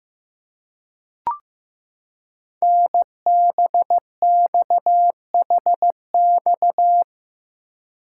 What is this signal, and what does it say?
Morse code, heard as a keyed tone, about 15 wpm, 700 Hz. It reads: NBXHX
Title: Code Groups - 5 Letters - 15wpm